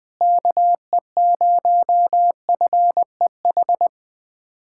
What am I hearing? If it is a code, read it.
Morse code, heard as a keyed tone, about 20 wpm, 700 Hz. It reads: KE0FEH
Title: USA Callsign DE Callsign 20wpm